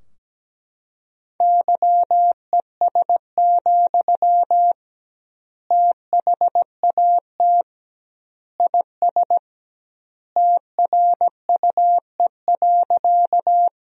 Morse code, heard as a keyed tone, about 17 wpm, 700 Hz. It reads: YES, THAT IS TRUE.
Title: 2x Word-Spacing - Sentences from Top 500 Words - 17wpm